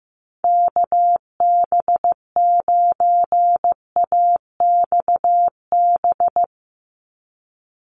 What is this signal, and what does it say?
Morse code, heard as a keyed tone, about 15 wpm, 700 Hz. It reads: KB9AXB
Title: USA Call Signs 15wpm